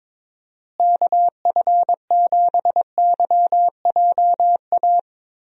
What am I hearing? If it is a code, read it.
Morse code, heard as a keyed tone, about 21 wpm, 700 Hz. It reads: KF7YJA
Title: USA Call Signs - 6 Characters - Repeated 3x - 22wpm